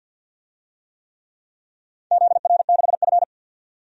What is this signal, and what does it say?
Morse code, heard as a keyed tone, about 49 wpm, 700 Hz. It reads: ZR6F
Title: DXCC-100 Call Signs - 4 Characters - 50wpm